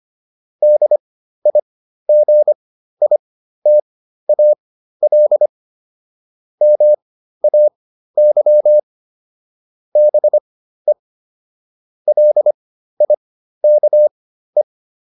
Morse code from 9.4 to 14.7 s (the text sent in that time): BE LIKE